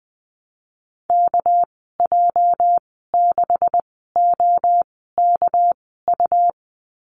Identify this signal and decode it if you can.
Morse code, heard as a keyed tone, about 20 wpm, spaced as for 14 wpm, 700 Hz. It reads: KJ6OKU
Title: 219 Lesson 35g - Number 0 - Call Signs 20wpm